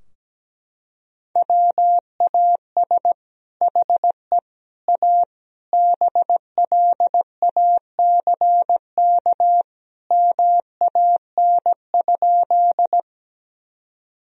Morse code, heard as a keyed tone, about 17 wpm, 700 Hz. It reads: WAS HE A BLACK MAN?
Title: Sentences from Top 1000 Words - Review Part 2 - 17wpm